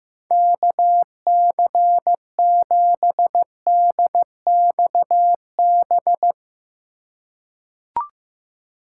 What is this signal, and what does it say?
Morse code, heard as a keyed tone, about 15 wpm, 700 Hz. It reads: KC7DXB
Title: USA Call Signs - 6 Characters - 4x Character Spacing - 15wpm